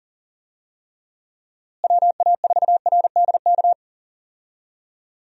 Morse code, heard as a keyed tone, about 37 wpm, 700 Hz. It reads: WA4RDK
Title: USA Call Signs - 6 Characters - Repeated 3x - 40wpm